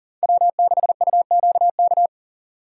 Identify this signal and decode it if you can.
Morse code, heard as a keyed tone, about 40 wpm, 700 Hz. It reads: W6UQK
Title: USA Call Signs - 4x Character Spacing - 40wpm